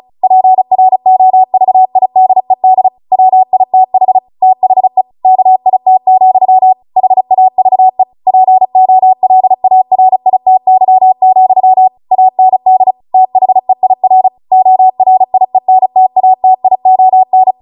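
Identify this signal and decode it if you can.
Morse code, heard as a keyed tone, about 36 wpm, 750 Hz. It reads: PROVIDED WITH THE KIT, HAVE POLARITY, AND THEIR ORIENTATION
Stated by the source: Amateur radio Morse code practice archive transmission at 35 wpm.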